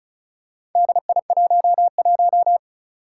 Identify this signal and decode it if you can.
Morse code, heard as a keyed tone, about 35 wpm, 700 Hz. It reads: DI11
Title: Four Challenging Letter-Number Combinations 35wpm